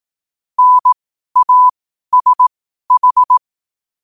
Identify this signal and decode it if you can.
Morse code, heard as a keyed tone, about 18 wpm, spaced as for 12 wpm, 1000 Hz. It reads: NASH
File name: qotdfiles / 12wpm